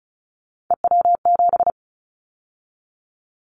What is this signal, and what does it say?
Morse code, heard as a keyed tone, about 34 wpm, 700 Hz. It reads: EW7
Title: Callsign Prefixes for Top 100 DXCC - 35wpm